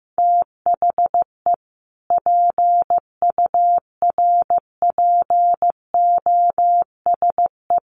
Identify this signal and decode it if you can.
Morse code, heard as a keyed tone, about 15 wpm, 700 Hz. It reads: THE PURPOSE
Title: Sets of 5 Words 15wpm